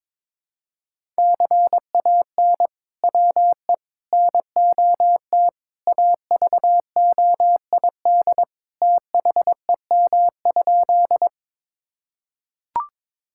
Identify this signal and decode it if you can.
Morse code, heard as a keyed tone, about 22 wpm, 700 Hz. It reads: CAN WE NOT AVOID THEM?
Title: Sentences from Top 1000 Words - Review Part 6 - 22wpm